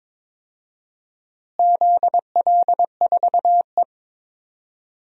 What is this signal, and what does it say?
Morse code, heard as a keyed tone, about 21 wpm, 700 Hz. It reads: ZL4E
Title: DXCC-100 Call Signs - 4 Characters - Repeated 2x - 22wpm